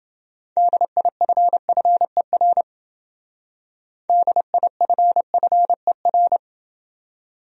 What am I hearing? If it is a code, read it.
Morse code, heard as a keyed tone, about 28 wpm, 700 Hz. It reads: DIFFER DIFFER